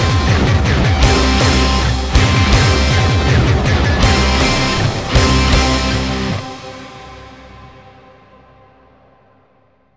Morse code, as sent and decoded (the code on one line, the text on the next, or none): none